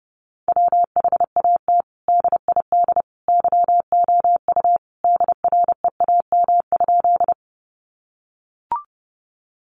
Morse code, heard as a keyed tone, about 30 wpm, 700 Hz. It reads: WHAT DID YOU DREAM?
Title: Sentences from Top 1500 Words 30wpm